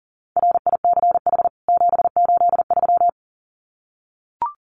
Morse code, heard as a keyed tone, about 40 wpm, 700 Hz. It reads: RICH 783
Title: CWT Exchange 40wpm